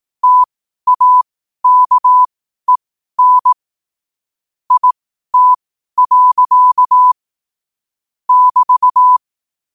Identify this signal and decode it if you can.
Morse code, heard as a keyed tone, about 18 wpm, spaced as for 12 wpm, 1000 Hz. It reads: TAKEN IT. <BT>
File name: qotdfiles / 12wpm